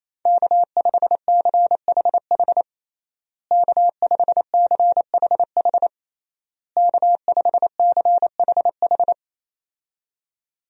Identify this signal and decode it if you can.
Morse code, heard as a keyed tone, about 28 wpm, 700 Hz. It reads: K5CHH K5CHH K5CHH
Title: USA Call Signs - 5 Characters - Repeated 3x - 28wpm